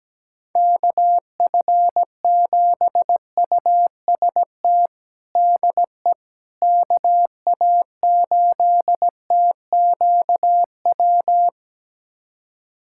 Morse code, heard as a keyed tone, about 17 wpm, 700 Hz. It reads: KF7UST DE KA8TQW
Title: USA Callsign DE Callsign 17wpm